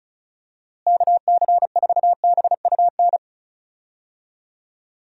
Morse code, heard as a keyed tone, about 36 wpm, 700 Hz. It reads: KC4BUN
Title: USA Call Signs - 6 Characters - Repeated 2x - 35wpm